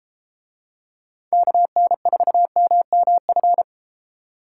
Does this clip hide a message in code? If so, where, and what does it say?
Yes: Morse code from 1.3 to 3.6 s, heard as a keyed tone, about 32 wpm, 700 Hz. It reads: KN4MMF